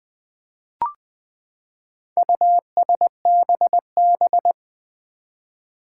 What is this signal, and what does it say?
Morse code, heard as a keyed tone, about 20 wpm, 700 Hz. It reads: USBB
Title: Four Challenging Letter-Number Combinations 20wpm